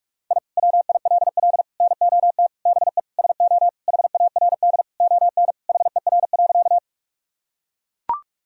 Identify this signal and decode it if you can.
Morse code, heard as a keyed tone, about 41 wpm, 700 Hz. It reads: I WILL NOT BE SO HARD ON HER.
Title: Sentences from Top 1000 Words - Review Part 1 - 45wpm